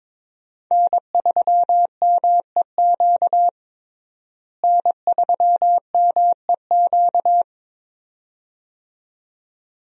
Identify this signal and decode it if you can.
Morse code, heard as a keyed tone, about 21 wpm, 700 Hz. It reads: N3MEQ N3MEQ
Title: USA Call Signs - 5 Characters - Repeated 2x - 22wpm